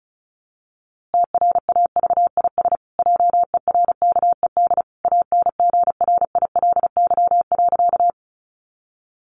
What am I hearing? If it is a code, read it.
Morse code, heard as a keyed tone, about 34 wpm, 700 Hz. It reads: TRAVIS JERKED ANGRILY.
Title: A Sound Of Thunder 35wpm